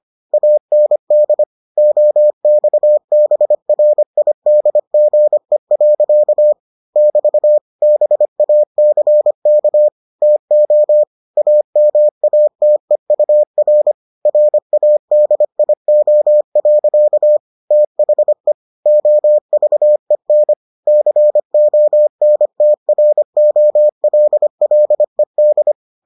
Morse code, heard as a keyed tone, about 25 wpm, 600 Hz. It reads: AND OXBRIDGE. <BT> BACK TO AMATEUR RADIO. THE OVEN CONTROLLED